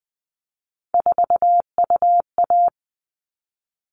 Morse code, heard as a keyed tone, about 20 wpm, 700 Hz. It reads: 4UA
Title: All Cut Numbers - 3-Digit 20wpm